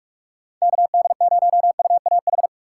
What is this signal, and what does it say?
Morse code, heard as a keyed tone, about 48 wpm, 700 Hz. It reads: KD0UAH